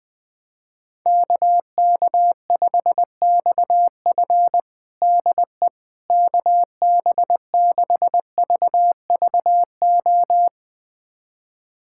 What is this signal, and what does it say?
Morse code, heard as a keyed tone, about 21 wpm, 700 Hz. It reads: KK5XF DE KB6VVO
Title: USA Callsign DE Callsign 20wpm